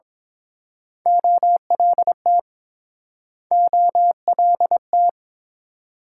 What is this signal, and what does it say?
Morse code, heard as a keyed tone, about 30 wpm, 700 Hz. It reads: OLT OLT